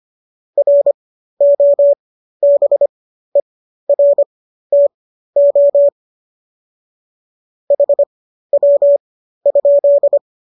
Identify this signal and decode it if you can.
Morse code, heard as a keyed tone, about 25 wpm, spaced as for 13 wpm, 570 Hz. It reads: ROBERTO HW?